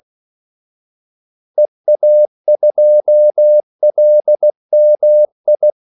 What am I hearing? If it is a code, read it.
Morse code, heard as a keyed tone, about 16 wpm, 600 Hz. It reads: EA2LMI
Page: Kaldesignaler 16 wpm